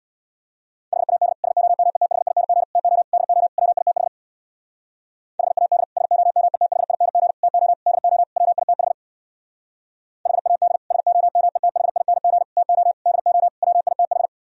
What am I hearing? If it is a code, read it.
Morse code, heard as a keyed tone, about 64 wpm, 690 Hz. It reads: HAD SOMETHING TO DO WITH HAD SOMETHING TO DO WITH HAD SOMETHING TO DO WITH